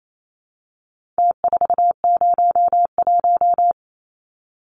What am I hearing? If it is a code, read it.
Morse code, heard as a keyed tone, about 28 wpm, 700 Hz. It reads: T401
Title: Numbers - Contest Style - 4-Digit 28wpm